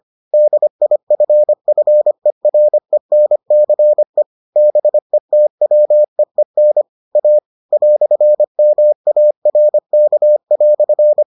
Morse code, heard as a keyed tone, about 25 wpm, 600 Hz. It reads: DIFFERENCE BETWEEN A "MARK"